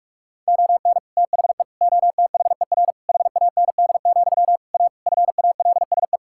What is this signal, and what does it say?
Morse code, heard as a keyed tone, about 46 wpm, 700 Hz. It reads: ON THE OTHER HAND, A FALSE
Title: Sentences from Top 1600 Words 45wpm